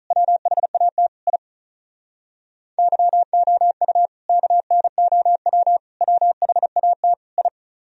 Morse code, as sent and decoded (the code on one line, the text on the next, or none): .-- .... .- - / .. / -.-- --- ..- / -.- -. --- .-- / .-- .... .- - / ..
WHAT I YOU KNOW WHAT I